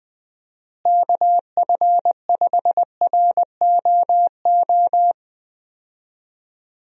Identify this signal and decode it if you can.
Morse code, heard as a keyed tone, about 19 wpm, 700 Hz. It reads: KF5ROO
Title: USA Call Signs - 6 Characters - Repeated 3x - 20wpm